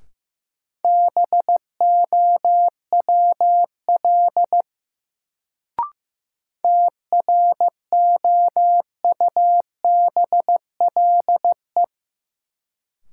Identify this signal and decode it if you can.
Morse code, heard as a keyed tone, about 15 wpm, 700 Hz. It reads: BOWL TROUBLE
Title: Sentences from Top 1200 Words - New Word Drill Alt - 15wpm